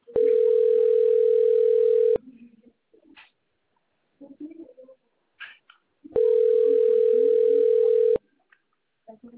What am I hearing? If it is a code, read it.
Ringback tone (North America), 440 + 480 Hz.